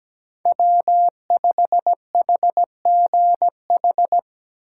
Morse code, heard as a keyed tone, about 20 wpm, 700 Hz. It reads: W5HGH
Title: Speed Racing - USA 1x3 Call Signs 17wpm